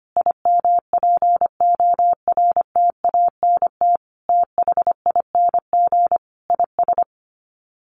Morse code, heard as a keyed tone, about 25 wpm, 700 Hz. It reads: IMPORTANT THING IS
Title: Sets of 5 Words Encore 25wpm